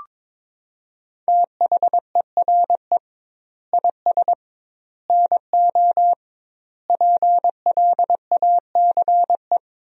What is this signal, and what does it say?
Morse code, heard as a keyed tone, about 22 wpm, 700 Hz. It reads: THERE IS NO PLACE
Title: 2x Word-Spacing - Sentences from Top 200 Words - 22wpm